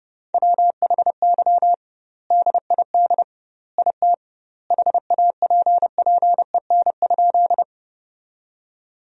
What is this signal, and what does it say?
Morse code, heard as a keyed tone, about 30 wpm, 700 Hz. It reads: WHY DID IT HAPPEN?
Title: 2x Word-Spacing - Sentences from Top 400 Words - 30wpm